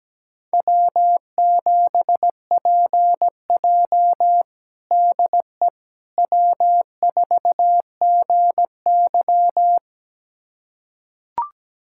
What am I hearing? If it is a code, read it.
Morse code, heard as a keyed tone, about 17 wpm, 700 Hz. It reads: W7PJ DE W4GY
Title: USA Callsign DE Callsign - 4 Characters - 17wpm